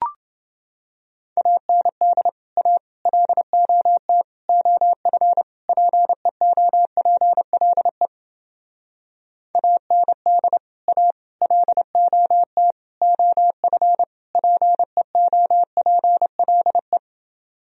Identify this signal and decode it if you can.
Morse code, heard as a keyed tone, about 29 wpm, 700 Hz. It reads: AND A LOT OF PEOPLE AND A LOT OF PEOPLE